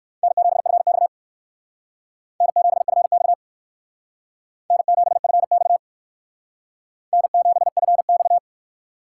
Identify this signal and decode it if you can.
Morse code, heard as a keyed tone, about 57 wpm, 700 Hz. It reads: N7FX N7FX N7FX N7FX